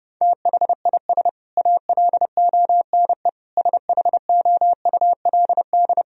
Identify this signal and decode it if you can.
Morse code, heard as a keyed tone, about 30 wpm, 700 Hz. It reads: THIS ALONE SHOULD
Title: Sentences from Top 1000 Words - Review Part 4 - 30wpm